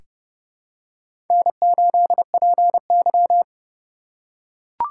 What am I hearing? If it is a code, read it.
Morse code, heard as a keyed tone, about 31 wpm, 700 Hz. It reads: N8PY
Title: USA Call Signs - 4 Characters - 2x Character Spacing - 30wpm